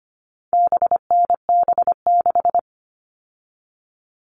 Morse code, heard as a keyed tone, about 25 wpm, 700 Hz. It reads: BNB6
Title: All Cut Numbers - 4-Digit 25wpm